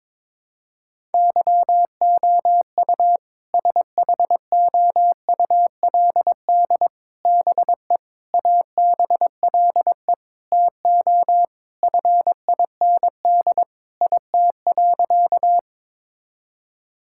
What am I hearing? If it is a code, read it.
Morse code, heard as a keyed tone, about 22 wpm, 700 Hz. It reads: YOU SHOULD BE ABLE TO FIND IT.
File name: practice / 22wpm / Sentences with 4-Word Phrases - 22wpm